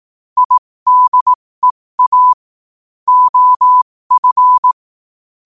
Morse code, heard as a keyed tone, about 18 wpm, spaced as for 15 wpm, 1000 Hz. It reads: IDEA OF
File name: qotdfiles / 15wpm